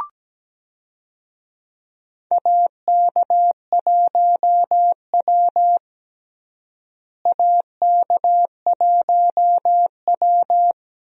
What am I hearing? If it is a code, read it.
Morse code, heard as a keyed tone, about 17 wpm, 700 Hz. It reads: AK1W AK1W
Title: USA Call Signs - 4 Characters - Repeated 3x - 17wpm